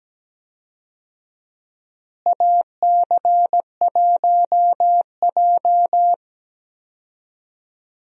Morse code, heard as a keyed tone, about 17 wpm, 700 Hz. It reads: AC1J
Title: USA Call Signs - 4 Characters - Repeated 3x - 17wpm